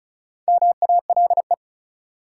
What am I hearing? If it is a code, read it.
Morse code, heard as a keyed tone, about 36 wpm, 700 Hz. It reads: MALE
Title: Four Letter Words - 35wpm